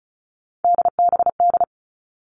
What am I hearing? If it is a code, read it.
Morse code, heard as a keyed tone, about 34 wpm, 700 Hz. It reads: DBD
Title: All Cut Numbers - 3-Digit 35wpm